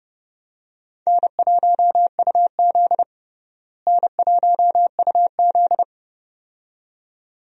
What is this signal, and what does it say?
Morse code, heard as a keyed tone, about 30 wpm, 700 Hz. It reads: N1UZ N1UZ
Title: USA Call Signs - 4 Characters - Repeated 2x - 30wpm